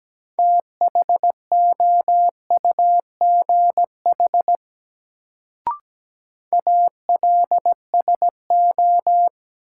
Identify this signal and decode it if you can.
Morse code, heard as a keyed tone, about 17 wpm, 700 Hz. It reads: THOUGH ALSO
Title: Sentences from Top 400 Words - New Word Drill - 17wpm